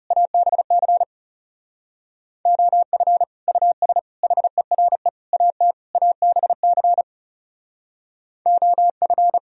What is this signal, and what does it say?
Morse code, heard as a keyed tone, about 37 wpm, 700 Hz. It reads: ABC OF US HERE AT ABC OF
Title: Speed-Racing - Sets of 5 Words 30wpm